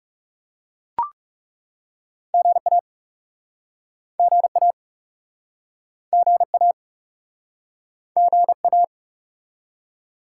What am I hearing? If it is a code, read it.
Morse code, heard as a keyed tone, about 42 wpm, 700 Hz. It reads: GA GA GA GA